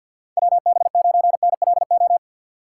Morse code, heard as a keyed tone, about 53 wpm, 700 Hz. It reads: WB9NLO
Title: USA Call Signs - 6 Characters - 6x Character Spacing - 50wpm